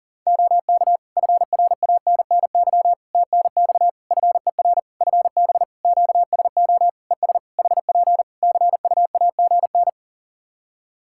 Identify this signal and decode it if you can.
Morse code, heard as a keyed tone, about 40 wpm, 700 Hz. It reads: OK FRANNY TNX FER FB QSO ES HP CUAGN
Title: Practice QSO - Exchange 4 - Ending - 40wpm